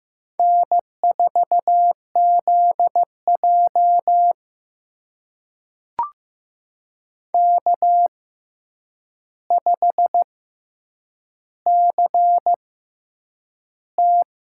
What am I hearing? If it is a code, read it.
Morse code, heard as a keyed tone, about 15 wpm, spaced as for 8 wpm, 700 Hz. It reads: N4ZJ K5CT
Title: USA Call Signs - 4 Characters - 6x Character Spacing - 15wpm